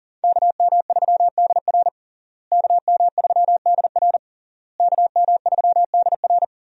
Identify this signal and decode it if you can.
Morse code, heard as a keyed tone, about 40 wpm, 700 Hz. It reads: KM3DR KM3DR KM3DR
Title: USA Call Signs - 5 Characters - Repeated 3x - 40wpm